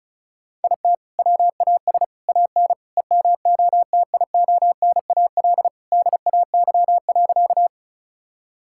Morse code, heard as a keyed tone, about 36 wpm, 700 Hz. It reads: IT WAS AN EMOTIONAL DAY.
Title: Sentences from Top 1000 Words - Review Part 5 - 35wpm